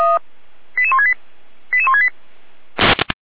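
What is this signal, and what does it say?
Five-tone selective call (ZVEI), heard as quick runs of five tones, tones of about 70 ms. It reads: 88168 88168
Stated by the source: Received on SDR as a ZVEI five-tone selective call, address 88247, from the Feuerwehr Orlamünde fire brigade server.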